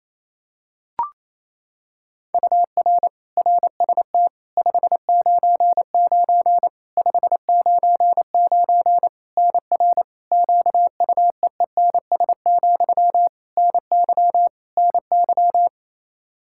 Morse code, heard as a keyed tone, about 28 wpm, 700 Hz. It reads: UR RST 599 599 NR QUEENS, NY NY
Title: K1USN SST QSO Practice 28wpm